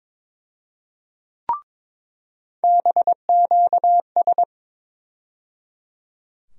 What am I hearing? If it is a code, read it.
Morse code, heard as a keyed tone, about 22 wpm, 700 Hz. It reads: BQS
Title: USA Call Sign Postfixes - 3 Letters 22wpm